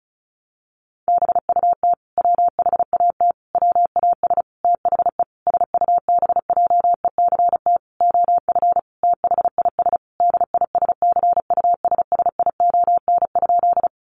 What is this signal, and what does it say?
Morse code, heard as a keyed tone, about 34 wpm, 700 Hz. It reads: BUT WHAT WAS THE SUBJECT OF THIS DISCUSSION?
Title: Sentences from Top 1100 Words 35wpm